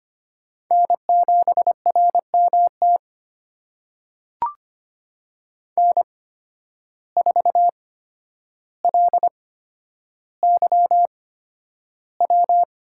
Morse code, heard as a keyed tone, about 25 wpm, spaced as for 11 wpm, 700 Hz. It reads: N7RMT N4LYW
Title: USA Call Signs - 5 Characters - 8x Character Spacing - 25wpm